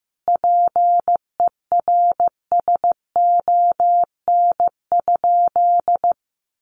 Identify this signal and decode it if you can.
Morse code, heard as a keyed tone, about 15 wpm, 700 Hz. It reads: PERSON?
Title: Sentences from Top 1300 Words 15wpm